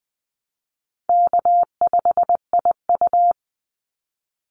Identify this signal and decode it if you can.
Morse code, heard as a keyed tone, about 20 wpm, 700 Hz. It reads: K5IU
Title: Call Signs Repeated 2x - 20wpm